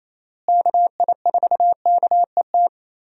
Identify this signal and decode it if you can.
Morse code, heard as a keyed tone, about 28 wpm, 700 Hz. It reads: KI4KET
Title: USA Call Signs - 6 Characters - Repeated 3x - 28wpm